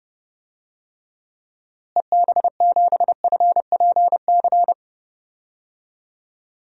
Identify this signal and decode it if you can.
Morse code, heard as a keyed tone, about 30 wpm, 700 Hz. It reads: EB7FPC
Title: DXCC-100 Call Signs - 6 Characters - Repeated 3x - 30wpm